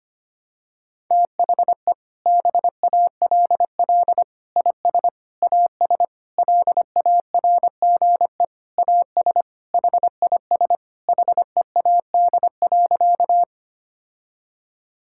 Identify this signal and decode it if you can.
Morse code, heard as a keyed tone, about 25 wpm, 700 Hz. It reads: THE BALL IS AS LARGE AS HIS HEAD.
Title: Sentences from Top 1000 Words - Review Part 6 - 25wpm